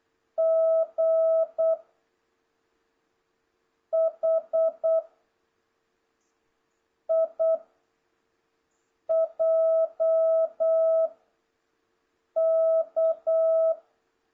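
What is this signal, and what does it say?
Morse code, heard as a keyed tone, about 8 wpm, 660 Hz. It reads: G H I J K
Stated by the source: Morse code lesson from JH8CBH's Zoom Morse training course (ZOOMモールス講習会), session 2 teaching letters G to K.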